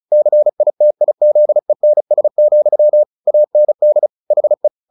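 Morse code, heard as a keyed tone, about 35 wpm, 600 Hz. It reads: CITIZENS, AND HE
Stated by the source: Morse practice at 35 wpm from CW med Gnister, sending Ken Follett's novel Never in English.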